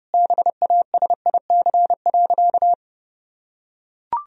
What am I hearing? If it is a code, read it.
Morse code, heard as a keyed tone, about 32 wpm, 700 Hz. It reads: BASIC.
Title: Sentences from Top 1000 Words - Review Part 3 - 30wpm